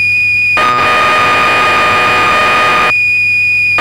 Cassette data (Kansas City type).
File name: magazines / Acorn User / 1984-11 BBC Programs